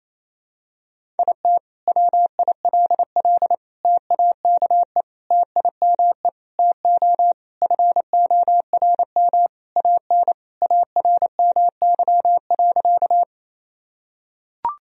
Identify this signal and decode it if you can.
Morse code, heard as a keyed tone, about 28 wpm, 700 Hz. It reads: IT WILL TAKE TIME TO FORM AN ARMY.
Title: Sentences from Top 1000 Words - Review Part 1 - 28wpm